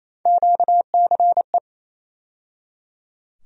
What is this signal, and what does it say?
Morse code, heard as a keyed tone, about 28 wpm, 700 Hz. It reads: QCE
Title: USA Call Sign Postfixes - 3 Letters 28wpm